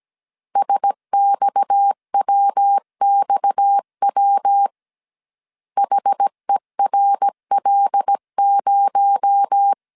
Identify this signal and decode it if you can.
Morse code, heard as a keyed tone, about 17 wpm, 800 Hz. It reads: SXWXW HERL0
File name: CW / DET / 1 6 1500C S A 16wpm 2635